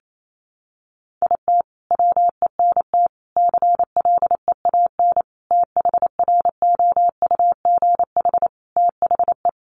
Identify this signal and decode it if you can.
Morse code, heard as a keyed tone, about 28 wpm, 700 Hz. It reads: IT WENT CLEAN THROUGH THE
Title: Sentences from Top 1200 Words 28wpm